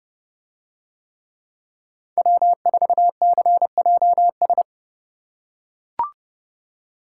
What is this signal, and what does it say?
Morse code, heard as a keyed tone, about 29 wpm, 700 Hz. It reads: W4CJS